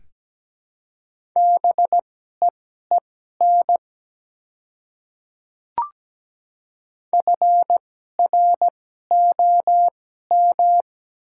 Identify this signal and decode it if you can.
Morse code, heard as a keyed tone, about 17 wpm, spaced as for 12 wpm, 700 Hz. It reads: BEEN FROM
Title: 2x Character Spacing - Top 200 Words - 17wpm